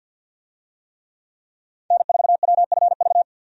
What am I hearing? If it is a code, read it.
Morse code, heard as a keyed tone, about 52 wpm, 700 Hz. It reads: N4WFV
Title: USA Call Signs - 5 Characters - 50wpm